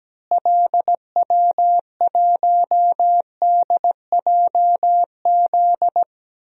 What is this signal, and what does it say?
Morse code, heard as a keyed tone, about 17 wpm, 700 Hz. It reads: LW1DJZ
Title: DXCC-100 Call Signs - 6 Characters - Repeated 2x - 17wpm